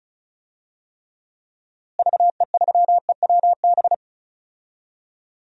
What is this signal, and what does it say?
Morse code, heard as a keyed tone, about 36 wpm, 700 Hz. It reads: VE3EWB
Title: DXCC-100 Call Signs - 6 Characters - Repeated 2x - 35wpm